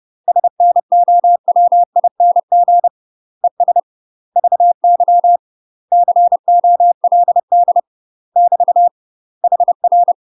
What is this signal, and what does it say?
Morse code, heard as a keyed tone, about 30 wpm, 700 Hz. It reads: SNOWING ES VY COLD <BT> HR
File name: UK MASTER QSOs TRAINING / 30WPM / UK MASTER QSOs 0027